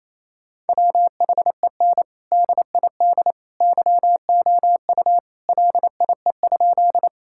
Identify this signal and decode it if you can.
Morse code, heard as a keyed tone, about 28 wpm, 700 Hz. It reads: WHEN DID YOU LIE?
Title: Sentences from Top 1600 Words 28wpm